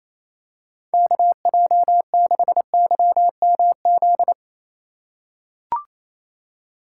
Morse code, heard as a keyed tone, about 28 wpm, 700 Hz. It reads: KJ6YMZ
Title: USA Call Signs - 6 Characters - 4x Character Spacing - 28wpm